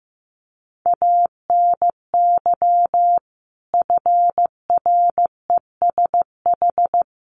Morse code, heard as a keyed tone, about 15 wpm, 700 Hz. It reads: ANY FRESH
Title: Sentences from Top 1200 Words 15wpm